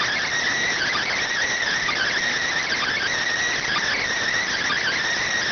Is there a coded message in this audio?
Yes: Emergency Alert System (SAME) tones (header bursts) from 0.6 to 5.0 s, heard as harsh buzzing bursts.